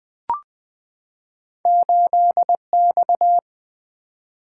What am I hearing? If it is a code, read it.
Morse code, heard as a keyed tone, about 20 wpm, 700 Hz. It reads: 8X
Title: Two Letter-Number Combinations - 20wpm